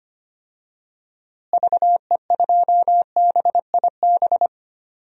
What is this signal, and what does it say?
Morse code, heard as a keyed tone, about 25 wpm, 700 Hz. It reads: VE2BIB
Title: DXCC-100 Call Signs - 6 Characters - Repeated 2x - 25wpm